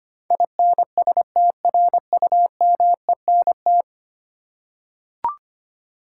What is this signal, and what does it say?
Morse code, heard as a keyed tone, about 25 wpm, 690 Hz. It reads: INSTRUMENT
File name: 10 Letter Words - First 3 Letters Given - 25wpm